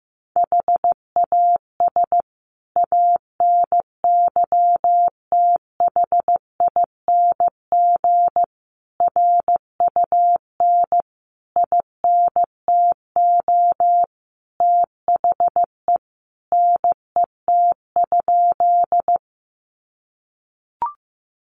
Morse code, heard as a keyed tone, about 14 wpm, 700 Hz. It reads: HAS ANYTHING RUN INTO THE NET?
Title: Sentences from Top 1400 Words 15wpm